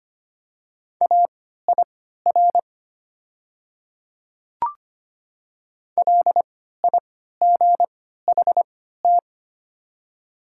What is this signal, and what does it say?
Morse code, heard as a keyed tone, about 25 wpm, spaced as for 14 wpm, 700 Hz. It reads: AIR LIGHT
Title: 3x Character Spacing - Top 300 Words - 25wpm